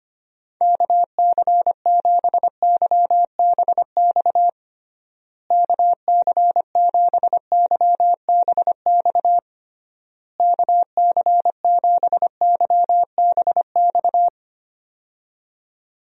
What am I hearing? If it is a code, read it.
Morse code, heard as a keyed tone, about 25 wpm, 700 Hz. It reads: KC7YBX KC7YBX KC7YBX
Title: USA Call Signs - 6 Characters - Repeated 3x - 25wpm